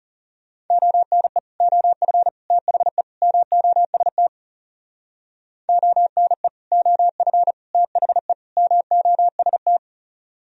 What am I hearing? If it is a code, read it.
Morse code, heard as a keyed tone, about 36 wpm, 700 Hz. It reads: ONE OF THE MOST ONE OF THE MOST
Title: Speed-Racing - Sets of 4 Words 30wpm